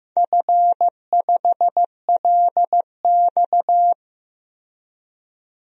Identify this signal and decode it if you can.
Morse code, heard as a keyed tone, about 15 wpm, 700 Hz. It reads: F5LX
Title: DXCC-100 Call Signs - 4 Characters - 15wpm